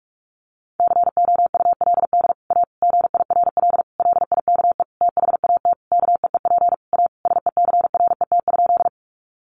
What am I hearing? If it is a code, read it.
Morse code, heard as a keyed tone, about 44 wpm, 700 Hz. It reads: COULD A GIRL LIKE THAT KEEP A SECRET?
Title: Sentences from Top 1200 Words 45wpm